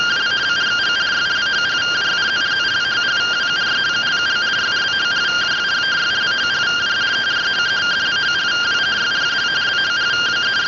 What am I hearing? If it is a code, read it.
Two-tone teleprinter signal, about 170 Hz shift.
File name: findings / axial iodine 10mm 184.gen-42 rtty rx